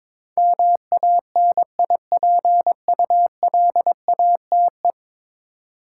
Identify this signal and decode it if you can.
Morse code, heard as a keyed tone, about 22 wpm, 700 Hz. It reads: MANIPULATE
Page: morse-code-ninja-practice-22wpm